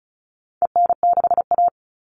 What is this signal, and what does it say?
Morse code, heard as a keyed tone, about 34 wpm, 700 Hz. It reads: EN6A
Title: All Cut Numbers - 4-Digit 35wpm